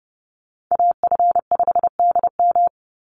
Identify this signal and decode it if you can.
Morse code, heard as a keyed tone, about 30 wpm, 700 Hz. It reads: AF5DM
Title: Call Signs Repeated 2x - 30wpm